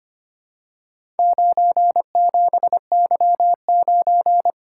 Morse code, heard as a keyed tone, about 25 wpm, 700 Hz. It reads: 97Y9
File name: practice / 25wpm / Four Letter-Number Combinations - 25wpm